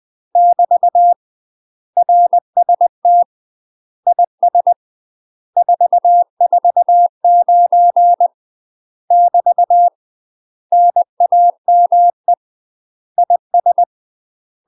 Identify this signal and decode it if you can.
Morse code, heard as a keyed tone, about 20 wpm, 700 Hz. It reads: <BT> RST IS 449 <BT> NAME IS
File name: UK MASTER QSOs TRAINING / 20WPM / UK MASTER QSOs 0059